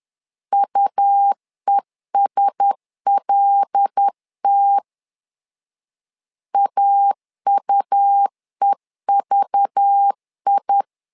Morse code, heard as a keyed tone, about 10 wpm, 800 Hz. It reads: UESLT AUEVI
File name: LEZIONI / 5 LBVC / 5 LBV 10wpm 2248